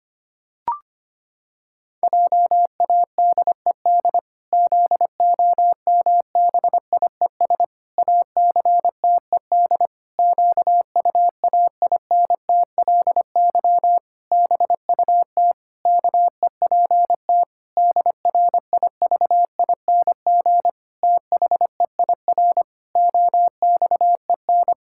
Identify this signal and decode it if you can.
Morse code, heard as a keyed tone, about 25 wpm, 700 Hz. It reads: JADED ZOMBIES ACTED QUAINTLY BUT KEPT DRIVING THEIR OXEN
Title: Pangram Challenge v2 - 25wpm